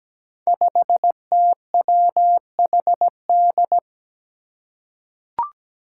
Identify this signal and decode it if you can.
Morse code, heard as a keyed tone, about 17 wpm, 700 Hz. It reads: 5TWHD
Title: Five Challenging Letter-Number Combinations 17wpm